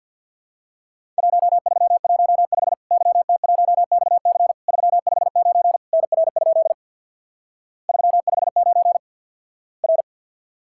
Morse code, heard as a keyed tone, about 50 wpm, 700 Hz. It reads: 1315 YT1XC 359 NR? 359 R
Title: Worked All Europe DX Contest - QTC Exchange with Repeats - 50wpm